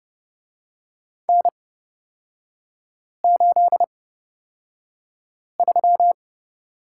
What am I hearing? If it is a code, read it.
Morse code, heard as a keyed tone, about 30 wpm, spaced as for 5 wpm, 700 Hz. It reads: N83
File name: Lesson 36e - BT - Three Characters 30wpm